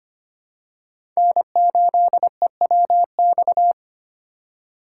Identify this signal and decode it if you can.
Morse code, heard as a keyed tone, about 25 wpm, 700 Hz. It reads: N8EWX
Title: K1USN SST QSO Practice - 25wpm